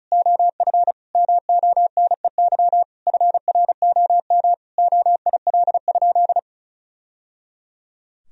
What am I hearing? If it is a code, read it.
Morse code, heard as a keyed tone, about 36 wpm, 700 Hz. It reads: OF MONEY FROM OIL?
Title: Sentences from Top 1000 Words - Review Part 4 - 35wpm